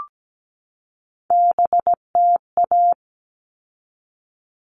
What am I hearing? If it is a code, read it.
Morse code, heard as a keyed tone, about 17 wpm, 700 Hz. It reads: BTA
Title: All Cut Numbers - 3-Digit 17wpm